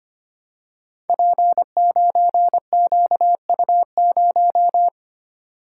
Morse code, heard as a keyed tone, about 25 wpm, 700 Hz. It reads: P9QU0
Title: Code Groups - 5 Letter:Numbers - 25wpm